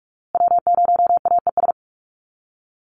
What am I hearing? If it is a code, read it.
Morse code, heard as a keyed tone, about 44 wpm, 700 Hz. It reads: W0AES
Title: USA Call Signs Repeated 3x 45wpm